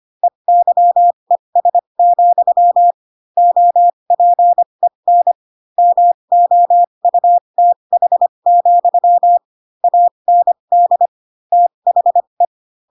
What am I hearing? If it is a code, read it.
Morse code, heard as a keyed tone, about 25 wpm, spaced as for 22 wpm, 700 Hz. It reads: EYES, OPEN MOUTH, AND THE